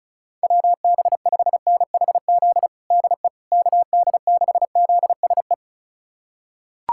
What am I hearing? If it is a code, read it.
Morse code, heard as a keyed tone, about 36 wpm, 700 Hz. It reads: WB5NHZ DE KD6ZSE
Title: USA Callsign DE Callsign 35wpm